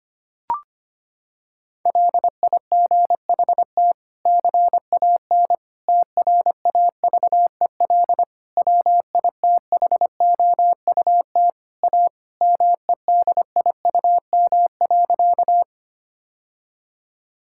Morse code, heard as a keyed tone, about 25 wpm, 700 Hz. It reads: LIGHT CAN TRAVEL WITHOUT A MEDIUM.
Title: Sentences from Top 1000 Words - Review Part 6 - 25wpm